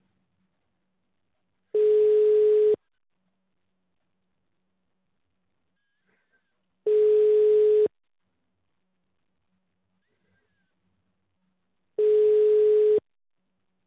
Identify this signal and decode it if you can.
Ringback tone (Europe, 425 Hz).